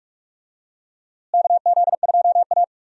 Morse code, heard as a keyed tone, about 46 wpm, 700 Hz. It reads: KZ2A
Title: USA Call Signs - 4 Characters - Repeated 2x - 45wpm